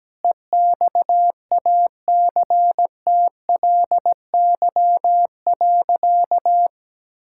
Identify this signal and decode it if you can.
Morse code, heard as a keyed tone, about 17 wpm, 700 Hz. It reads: EXACTLY.
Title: Sentences from Top 1000 Words - Review Part 6 - 17wpm